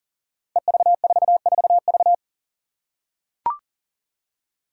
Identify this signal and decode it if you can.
Morse code, heard as a keyed tone, about 42 wpm, 700 Hz. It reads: EV44V
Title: Five Challenging Letter-Number Combinations 40wpm